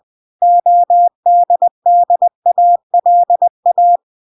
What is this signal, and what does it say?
Morse code, heard as a keyed tone, about 20 wpm, 700 Hz. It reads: ODDALA